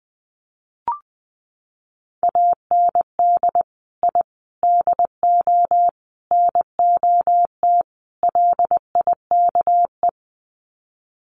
Morse code, heard as a keyed tone, about 20 wpm, 700 Hz. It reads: AND I DO NOT LIKE
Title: Sets of 5 Words Encore2 20wpm